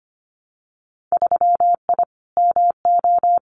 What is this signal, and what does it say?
Morse code, heard as a keyed tone, about 25 wpm, 700 Hz. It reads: 3I MO
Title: Winter Field Day Exchanges 25wpm